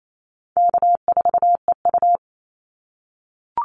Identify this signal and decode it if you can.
Morse code, heard as a keyed tone, about 28 wpm, 700 Hz. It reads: K4EU
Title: Easy Call Signs - 28wpm